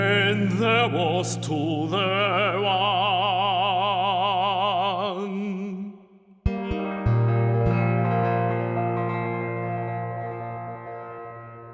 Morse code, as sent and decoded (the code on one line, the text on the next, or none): none